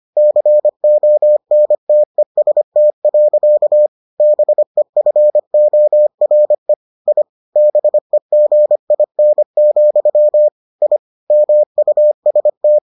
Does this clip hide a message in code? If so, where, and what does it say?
Yes: Morse code from 0.2 to 12.8 s, heard as a keyed tone, about 25 wpm, 600 Hz. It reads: CONTEST. BEFORE I BEGIN, I MUST